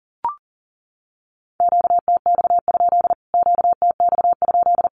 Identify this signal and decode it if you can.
Morse code, heard as a keyed tone, about 40 wpm, 700 Hz. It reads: QTX? QTX?
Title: Q-Codes Repeated 3x 40wpm